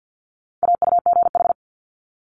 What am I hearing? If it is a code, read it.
Morse code, heard as a keyed tone, about 50 wpm, 700 Hz. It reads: AUGH
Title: Multi-letter Phonemes Quick Fire 50wpm